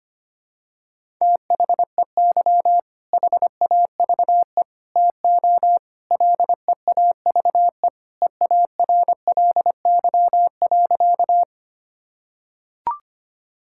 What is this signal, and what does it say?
Morse code, heard as a keyed tone, about 25 wpm, 700 Hz. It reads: THEY HAVE TO LEAVE EARLY.
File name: practice / 25wpm / Sentences with 3-Word Phrases - 25wpm